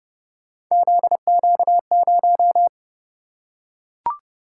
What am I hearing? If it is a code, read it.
Morse code, heard as a keyed tone, about 28 wpm, 700 Hz. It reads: ZQ0